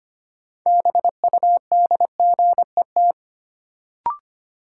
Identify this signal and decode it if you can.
Morse code, heard as a keyed tone, about 25 wpm, 700 Hz. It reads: BUDGET
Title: Sentences from Top 1200 Words - New Word Drill Alt - 25wpm